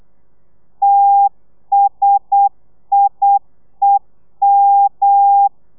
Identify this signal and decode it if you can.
Morse code, heard as a keyed tone, about 8 wpm, 800 Hz. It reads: TSIEM
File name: telegrafia / i2tfj / Lez1 35